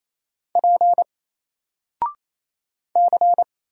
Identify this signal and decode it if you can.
Morse code, heard as a keyed tone, about 28 wpm, 700 Hz. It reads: P C
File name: Single-Letter-Number-28wpm